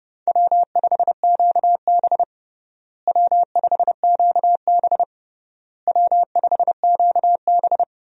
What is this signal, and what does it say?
Morse code, heard as a keyed tone, about 30 wpm, 700 Hz. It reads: W5QB W5QB W5QB
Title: USA Call Signs - 4 Characters - Repeated 3x - 30wpm